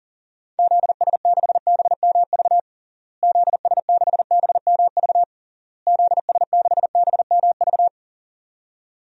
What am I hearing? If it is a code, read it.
Morse code, heard as a keyed tone, about 40 wpm, 700 Hz. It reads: ZS6BMV ZS6BMV ZS6BMV
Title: DXCC-100 Call Signs - 6 Characters - Repeated 3x - 40wpm